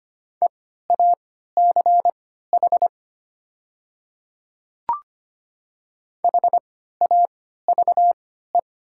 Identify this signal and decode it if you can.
Morse code, heard as a keyed tone, about 25 wpm, spaced as for 14 wpm, 700 Hz. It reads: EACH HAVE
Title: 3x Character Spacing - Top 100 Words - 25wpm